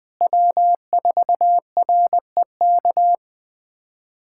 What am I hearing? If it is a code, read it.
Morse code, heard as a keyed tone, about 20 wpm, 700 Hz. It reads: W4REK
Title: USA Callsign DE Callsign 20wpm